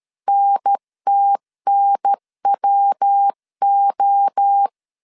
Morse code, heard as a keyed tone, about 13 wpm, 800 Hz. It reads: NTNWO
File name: CW / DET / 1 3 1500C S A 12wpm 2905